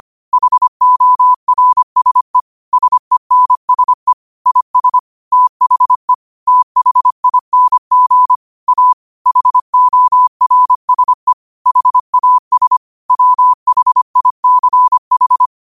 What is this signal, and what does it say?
Morse code, heard as a keyed tone, about 25 wpm, 1000 Hz. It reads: HORSE SENSE IS THE THING A HORSE HAS WHICH
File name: qotdfiles / 25wpm